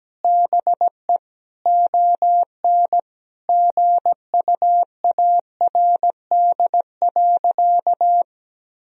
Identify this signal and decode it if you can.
Morse code, heard as a keyed tone, about 17 wpm, 700 Hz. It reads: BE ON GUARD.
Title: Sentences from Top 1000 Words - Review Part 4 - 17wpm